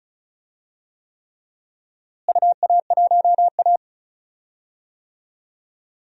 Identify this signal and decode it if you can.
Morse code, heard as a keyed tone, about 36 wpm, 700 Hz. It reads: UA1A
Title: DXCC-100 Call Signs - 4 Characters - Repeated 3x - 35wpm